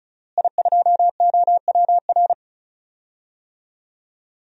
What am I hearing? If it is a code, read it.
Morse code, heard as a keyed tone, about 36 wpm, 700 Hz. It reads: I2OWR
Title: DXCC-100 Call Signs - 5 Characters - Repeated 2x - 35wpm